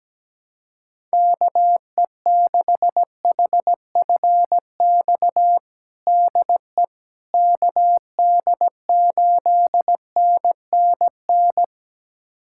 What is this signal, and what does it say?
Morse code, heard as a keyed tone, about 17 wpm, 700 Hz. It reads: KE6HFX DE KD8NNN
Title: USA Callsign DE Callsign - 6 Characters - 17wpm